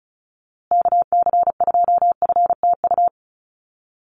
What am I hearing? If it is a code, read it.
Morse code, heard as a keyed tone, about 34 wpm, 700 Hz. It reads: KC2FTU
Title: USA Call Signs Repeated 2x 35wpm